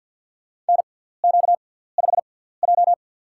Morse code, heard as a keyed tone, about 53 wpm, spaced as for 17 wpm, 700 Hz. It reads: NQ5J
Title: USA Call Signs - 4 Characters - 6x Character Spacing - 50wpm